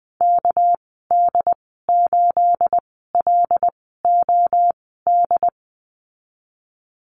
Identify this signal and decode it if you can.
Morse code, heard as a keyed tone, about 20 wpm, spaced as for 14 wpm, 700 Hz. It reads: KD8LOD
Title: 231 Lesson 37g - Letter x - Call Signs 20wpm